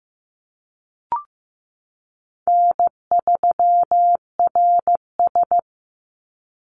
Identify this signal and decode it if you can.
Morse code, heard as a keyed tone, about 15 wpm, 700 Hz. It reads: N3RS
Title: Easy Call Signs Repeated 2x - 15wpm